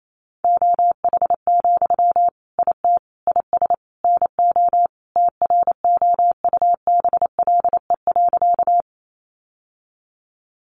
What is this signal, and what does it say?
Morse code, heard as a keyed tone, about 28 wpm, 700 Hz. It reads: OH, IT IS NO TROUBLE.
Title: Sentences from Top 1200 Words 28wpm